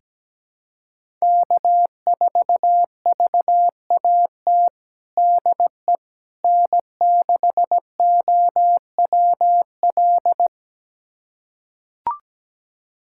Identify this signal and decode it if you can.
Morse code, heard as a keyed tone, about 17 wpm, 700 Hz. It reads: K4VAT DE N6OWL
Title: USA Callsign DE Callsign - 5 Characters - 17wpm